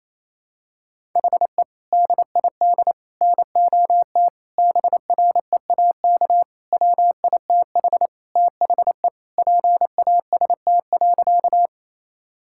Morse code, heard as a keyed tone, about 27 wpm, 700 Hz. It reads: HE DID NOT BREAK WITH THE PAST.
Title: Sentences from Top 1000 Words - Review Part 6 - 28wpm